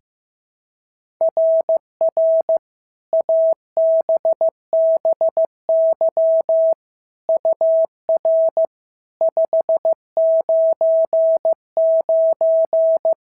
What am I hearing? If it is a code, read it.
Morse code, heard as a keyed tone, about 15 wpm, 690 Hz. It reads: RR ABBY UR 599
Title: SOTA QSO Practice 15wpm